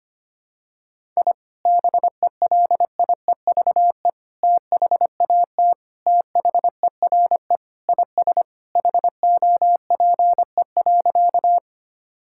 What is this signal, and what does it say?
Morse code, heard as a keyed tone, about 25 wpm, 700 Hz. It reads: I BELIEVE THAT THERE IS HOPE.
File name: practice / 25wpm / Sentences with 3-Word Phrases - 25wpm